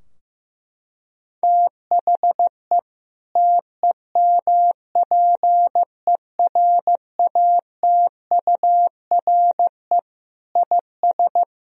Morse code, heard as a keyed tone, about 15 wpm, 700 Hz. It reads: THE TEMPERATURE IS
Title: Sentences from Top 1000 Words - Review Part 4 - 15wpm